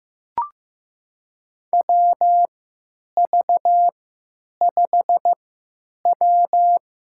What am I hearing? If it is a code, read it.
Morse code, heard as a keyed tone, about 15 wpm, spaced as for 9 wpm, 700 Hz. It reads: WV5W
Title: USA Call Signs - 3x Character Spacing - 15wpm